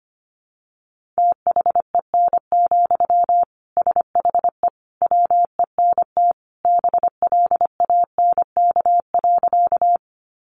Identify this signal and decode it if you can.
Morse code, heard as a keyed tone, about 25 wpm, 700 Hz. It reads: THEN, SHE WENT BLANK.
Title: Sentences from Top 1500 Words 25wpm